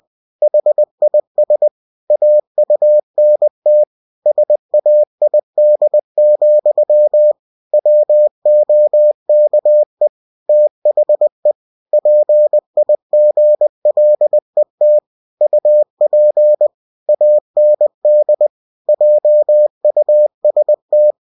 Morse code, heard as a keyed tone, about 20 wpm, 600 Hz. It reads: HIS AUNT SAID, WOKE THE PIGLET UP AND JUST
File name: cw-books / winnie-the-pooh / winnie-20wpm-page-0101